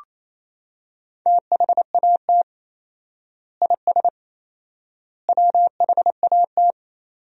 Morse code, heard as a keyed tone, about 28 wpm, 700 Hz. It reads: THAT IS WHAT
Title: 4x Word-Spacing - Sentences from Top 100 Words - 28wpm